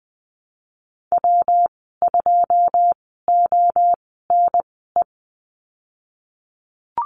Morse code, heard as a keyed tone, about 20 wpm, spaced as for 15 wpm, 700 Hz. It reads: W2ONE
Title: 122 Lesson 20g - Number 3 - Call Signs 20wpm